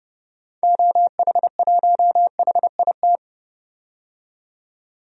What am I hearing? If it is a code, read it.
Morse code, heard as a keyed tone, about 30 wpm, 700 Hz. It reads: OH1HIT
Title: DXCC-100 Call Signs - 6 Characters - 30wpm